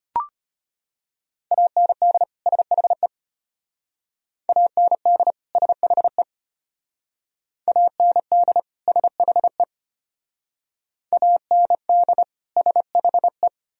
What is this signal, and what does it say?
Morse code, heard as a keyed tone, about 32 wpm, 700 Hz. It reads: AND SHE AND SHE AND SHE AND SHE